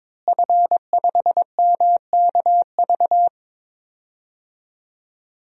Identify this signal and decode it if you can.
Morse code, heard as a keyed tone, about 22 wpm, 700 Hz. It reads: F5MKV
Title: DXCC-100 Call Signs - 5 Characters - Repeated 3x - 22wpm